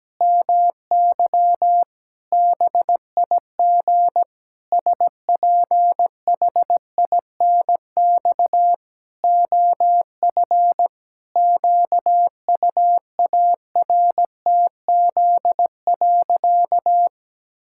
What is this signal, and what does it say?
Morse code, heard as a keyed tone, about 17 wpm, 700 Hz. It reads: MY BIG SPHINX OF QUARTZ.
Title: Pangram Challenge v2 - 17wpm